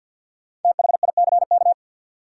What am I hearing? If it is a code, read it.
Morse code, heard as a keyed tone, about 48 wpm, 700 Hz. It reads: THICK
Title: Sentences from Top 1600 Words - New Word Drill - 50wpm